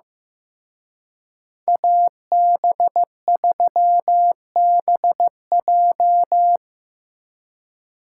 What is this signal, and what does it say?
Morse code, heard as a keyed tone, about 18 wpm, 700 Hz. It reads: AB3BJ
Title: Speed Racing - USA 2x2 Call Signs 15wpm